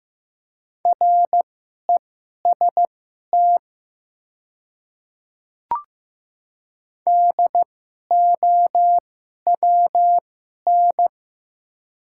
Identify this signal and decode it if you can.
Morse code, heard as a keyed tone, about 15 wpm, spaced as for 11 wpm, 700 Hz. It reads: REST DOWN
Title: 2x Character Spacing - Top 500 Words - 15wpm